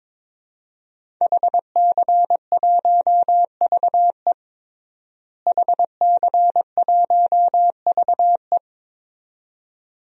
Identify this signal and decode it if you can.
Morse code, heard as a keyed tone, about 22 wpm, 700 Hz. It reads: HC1VE HC1VE
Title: DXCC-100 Call Signs - 5 Characters - Repeated 2x - 22wpm